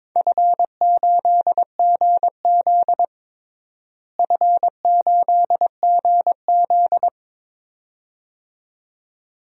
Morse code, heard as a keyed tone, about 22 wpm, 700 Hz. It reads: F8GZ F8GZ
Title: DXCC-100 Call Signs - 4 Characters - Repeated 2x - 22wpm